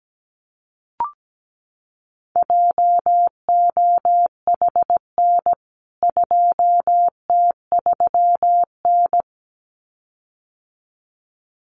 Morse code, heard as a keyed tone, about 17 wpm, 700 Hz. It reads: JOHN 2T3N
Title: CWT Exchange 17wpm